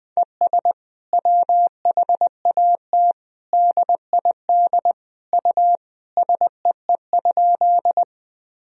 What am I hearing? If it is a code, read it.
Morse code, heard as a keyed tone, about 21 wpm, 700 Hz. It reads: ES WHAT DID U SEE?
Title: Sentences from Top 100 Words and Abbreviations 20wpm